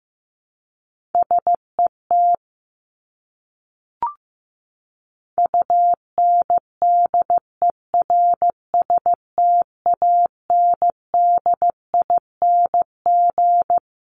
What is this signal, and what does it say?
Morse code, heard as a keyed tone, about 15 wpm, 700 Hz. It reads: SET UNDERSTANDING
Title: ING Suffix - Top 100 Words with Base Words 15wpm